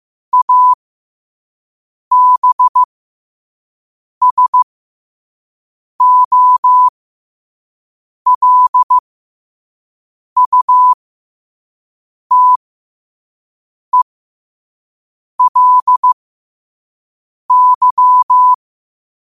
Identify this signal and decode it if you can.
Morse code, heard as a keyed tone, about 15 wpm, spaced as for 5 wpm, 1000 Hz. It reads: ABSOLUTELY
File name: qotdfiles / 5wpm